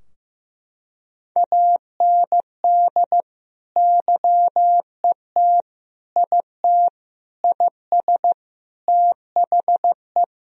Morse code, heard as a keyed tone, about 14 wpm, 700 Hz. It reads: AND YET IT IS THE
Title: Sentences from Top 1000 Words - Review Part 3 - 15wpm